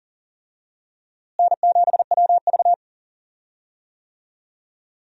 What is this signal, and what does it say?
Morse code, heard as a keyed tone, about 38 wpm, 700 Hz. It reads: N7WV
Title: USA Call Signs - 4 Characters - Repeated 2x - 40wpm